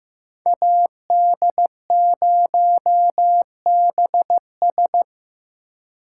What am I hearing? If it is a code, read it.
Morse code, heard as a keyed tone, about 18 wpm, 700 Hz. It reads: AD0BS
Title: Speed Racing - USA Call Signs 15wpm